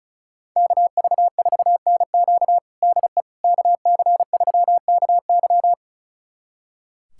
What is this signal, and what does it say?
Morse code, heard as a keyed tone, about 34 wpm, 700 Hz. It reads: KV4NQ DE KC3KY
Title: USA Callsign DE Callsign 35wpm